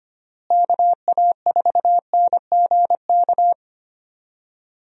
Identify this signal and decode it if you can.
Morse code, heard as a keyed tone, about 25 wpm, 700 Hz. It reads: KA4NGK
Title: USA Call Signs - 6 Characters - Repeated 3x - 25wpm